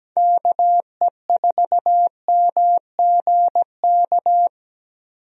Morse code, heard as a keyed tone, about 17 wpm, 700 Hz. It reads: KE4MGK
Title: USA Call Signs - 6 Characters - 17wpm